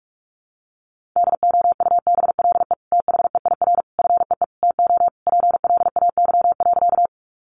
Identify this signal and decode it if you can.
Morse code, heard as a keyed tone, about 44 wpm, 700 Hz. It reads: DOUBLE THEIR FEE TO PLAY.
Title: Sentences from Top 1500 Words 45wpm